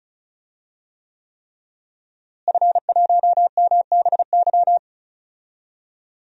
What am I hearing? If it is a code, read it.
Morse code, heard as a keyed tone, about 36 wpm, 700 Hz. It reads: F1MBY
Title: DXCC-100 Call Signs - 5 Characters - Repeated 3x - 35wpm